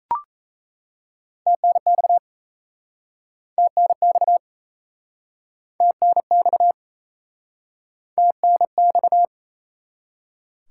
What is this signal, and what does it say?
Morse code, heard as a keyed tone, about 38 wpm, 700 Hz. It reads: TNX TNX TNX TNX